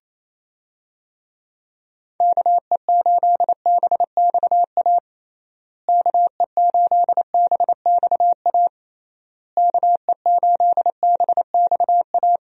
Morse code, heard as a keyed tone, about 28 wpm, 700 Hz. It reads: KE8BXA KE8BXA KE8BXA
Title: USA Call Signs - 6 Characters - Repeated 3x - 28wpm